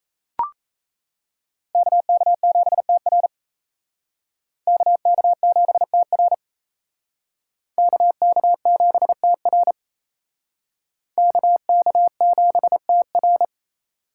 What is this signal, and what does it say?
Morse code, heard as a keyed tone, about 35 wpm, 700 Hz. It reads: KK7TR KK7TR KK7TR KK7TR
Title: Speed Racing - USA 2x2 Call Signs 28wpm